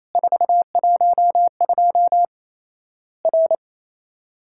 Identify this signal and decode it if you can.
Morse code, heard as a keyed tone, about 27 wpm, 700 Hz. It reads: 412 R
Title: Worked All Europe DX Contest - QTC Exchange - 28wpm